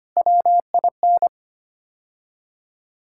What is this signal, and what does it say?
Morse code, heard as a keyed tone, about 25 wpm, 700 Hz. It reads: WIN
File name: practice / 25wpm / Sentences with 3-Word Phrases - 25wpm